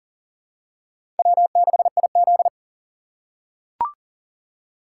Morse code, heard as a keyed tone, about 37 wpm, 700 Hz. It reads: W6IZ
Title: USA Call Signs - 3x Character Spacing - 40wpm